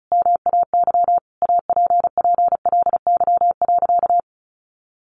Morse code, heard as a keyed tone, about 34 wpm, 700 Hz. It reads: MAY APPLY.
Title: Sentences from Top 1400 Words 35wpm